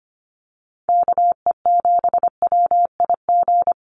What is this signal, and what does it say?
Morse code, heard as a keyed tone, about 25 wpm, 700 Hz. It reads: KE7WIG
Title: USA Call Signs 25wpm